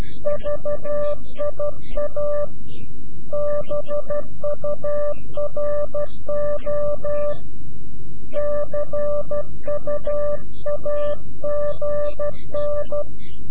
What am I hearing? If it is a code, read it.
Morse code, heard as a keyed tone, about 12 wpm, 590 Hz. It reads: VIA BURO CUAGN
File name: CW-QSO / CQ Call / CQ-QSO